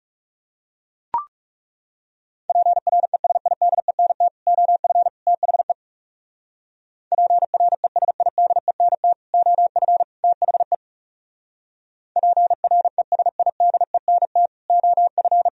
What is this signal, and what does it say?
Morse code, heard as a keyed tone, about 37 wpm, 700 Hz. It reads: PRESIDENT OF THE PRESIDENT OF THE PRESIDENT OF
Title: Speed-Racing - Sets of 3 Words 30wpm